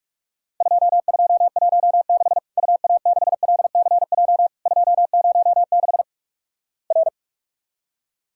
Morse code, heard as a keyed tone, about 44 wpm, 700 Hz. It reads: 2216 UA6LCJ 206 R
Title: Worked All Europe DX Contest - QTC Exchange - 45wpm